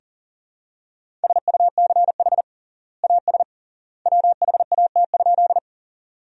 Morse code, heard as a keyed tone, about 40 wpm, 700 Hz. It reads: SUCH AS WHAT?
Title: 3x Word-Spacing - Sentences from Top 200 Words - 40wpm